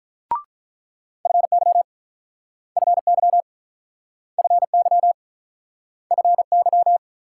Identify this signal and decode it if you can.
Morse code, heard as a keyed tone, about 47 wpm, 700 Hz. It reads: FY FY FY FY